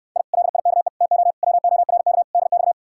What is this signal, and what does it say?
Morse code, heard as a keyed tone, about 69 wpm, 700 Hz. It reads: I LIKE TO WORK DX